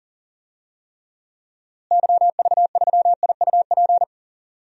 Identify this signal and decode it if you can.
Morse code, heard as a keyed tone, about 40 wpm, 700 Hz. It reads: YV3IUP
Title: DXCC-100 Call Signs - 6 Characters - Repeated 3x - 40wpm